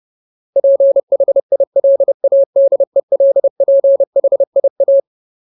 Morse code, heard as a keyed tone, about 30 wpm, 550 Hz. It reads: PHILADELPHIA
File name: Ultimate Word List / 30WPM Ultimate